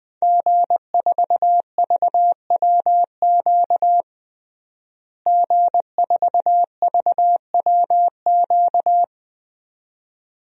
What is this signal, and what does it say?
Morse code, heard as a keyed tone, about 20 wpm, 700 Hz. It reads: G4VWQ G4VWQ
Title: DXCC-100 Call Signs - 5 Characters - Repeated 2x - 20wpm